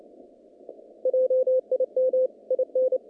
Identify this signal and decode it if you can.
Morse code, heard as a keyed tone, about 28 wpm, 520 Hz. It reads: JIM IN
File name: contests / naqp-cw-k5tr-op-k5pi- -2200